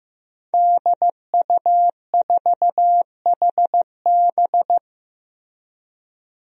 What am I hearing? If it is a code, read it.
Morse code, heard as a keyed tone, about 15 wpm, 700 Hz. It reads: DU4HB
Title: DXCC-100 Call Signs - 5 Characters - Repeated 3x - 15wpm